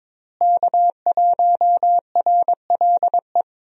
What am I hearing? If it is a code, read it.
Morse code, heard as a keyed tone, about 22 wpm, 700 Hz. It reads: K1RLE
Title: USA Call Signs - 5 Characters - Repeated 3x - 22wpm